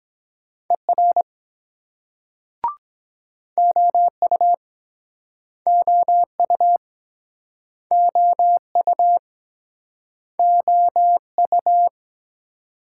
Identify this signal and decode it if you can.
Morse code, heard as a keyed tone, about 24 wpm, 700 Hz. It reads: ER OU OU OU OU